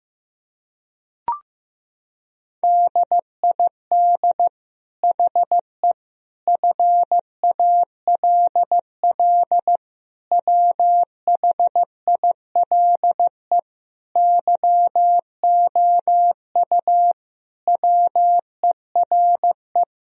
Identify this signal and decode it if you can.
Morse code, heard as a keyed tone, about 15 wpm, 700 Hz. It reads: DID HE FALL WHILE YOU WERE
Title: Sentences from Top 1000 Words - Review Part 2 - 15wpm